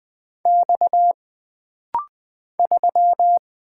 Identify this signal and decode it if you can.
Morse code, heard as a keyed tone, about 20 wpm, 700 Hz. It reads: X 3
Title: Single Letter-Number - Rapid-Fire 20wpm